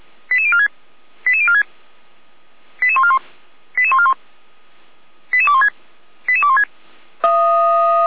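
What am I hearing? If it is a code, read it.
Five-tone selective call (ZVEI), heard as quick runs of five tones, tones of about 70 ms. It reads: 88046 88046 88141 88141 88216 88216
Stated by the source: ZVEI five-tone selective call received by SDR.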